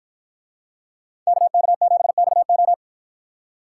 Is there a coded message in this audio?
Yes: Morse code from 1.3 to 2.7 s, heard as a keyed tone, about 51 wpm, 700 Hz. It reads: KK7XO